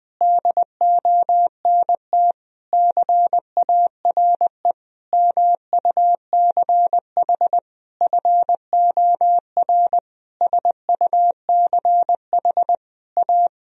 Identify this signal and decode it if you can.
Morse code, heard as a keyed tone, about 20 wpm, 700 Hz. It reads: DONT CARE MUCH FOR SUCH A
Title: Sentences from Top 1000 Words - Review Part 4 - 20wpm